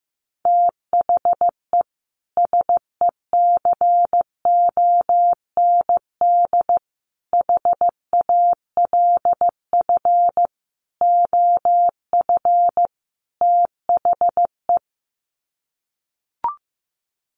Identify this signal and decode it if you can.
Morse code, heard as a keyed tone, about 15 wpm, 700 Hz. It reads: THE SECOND HALF OF THE
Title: Sets of 5 Words 15wpm